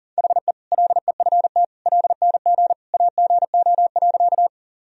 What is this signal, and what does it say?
Morse code, heard as a keyed tone, about 40 wpm, 700 Hz. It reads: HE LEFT LNG AGO.
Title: Sentences from Top 500 Words and Abbreviations 40wpm